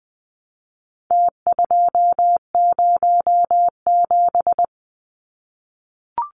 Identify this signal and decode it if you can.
Morse code, heard as a keyed tone, about 20 wpm, 700 Hz. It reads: T207
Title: Numbers - Contest Style - 4-Digit 20wpm